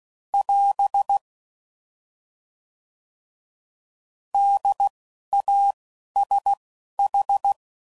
Morse code, heard as a keyed tone, about 16 wpm, 800 Hz. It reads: <AS> D A S H
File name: Prefixes / By First Letter / Ds